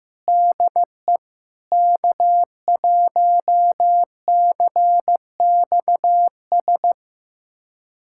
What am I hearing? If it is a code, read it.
Morse code, heard as a keyed tone, about 16 wpm, 700 Hz. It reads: DE K1CXS
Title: USA Callsign DE Callsign 15wpm